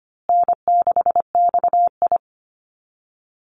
Morse code, heard as a keyed tone, about 25 wpm, 700 Hz. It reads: N6XI
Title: Call Signs Repeated 2x - 25wpm